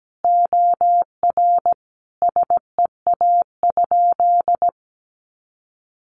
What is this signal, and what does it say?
Morse code, heard as a keyed tone, about 17 wpm, 700 Hz. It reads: OR SEA?
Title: Sentences from Top 500 Words 17wpm